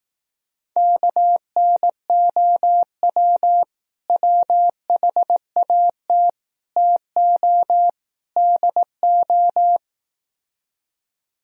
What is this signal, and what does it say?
Morse code, heard as a keyed tone, about 18 wpm, 700 Hz. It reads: KNOW WHAT TO DO
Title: Speed-Racing - Sets of 4 Words 15wpm